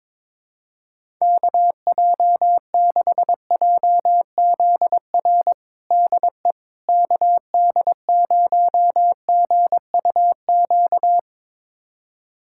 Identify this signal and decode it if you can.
Morse code, heard as a keyed tone, about 22 wpm, 700 Hz. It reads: KJ6JZR DE KD0GUQ
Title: USA Callsign DE Callsign 22wpm